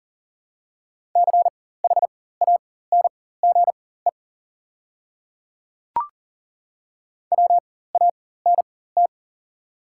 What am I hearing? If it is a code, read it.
Morse code, heard as a keyed tone, about 40 wpm, spaced as for 19 wpm, 700 Hz. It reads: CHANGE WANT
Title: 4x Character Spacing - Top 200 Words - 40wpm